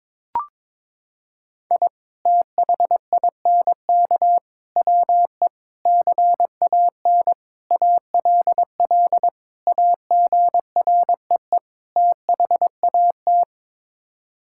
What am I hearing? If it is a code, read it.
Morse code, heard as a keyed tone, about 23 wpm, 700 Hz. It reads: I THINK WE CAN ALL AGREE THAT
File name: practice / 22wpm / Sets of 7 Words - 22wpm